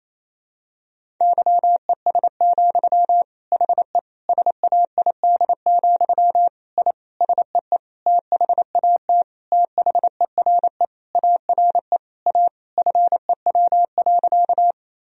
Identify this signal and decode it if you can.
Morse code, heard as a keyed tone, about 28 wpm, 700 Hz. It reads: YES, HE SAID, I SEE THAT THERE ARE A FEW.
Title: Sentences from Top 1000 Words - Review Part 1 - 28wpm